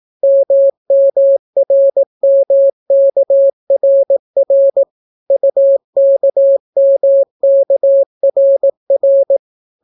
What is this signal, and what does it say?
Morse code, heard as a keyed tone, about 18 wpm, 550 Hz. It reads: MMRMKRR UKMKRR